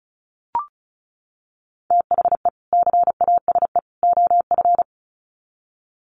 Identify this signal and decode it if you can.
Morse code, heard as a keyed tone, about 34 wpm, 700 Hz. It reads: THE CASE OF
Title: Sets of 3 Words 35wpm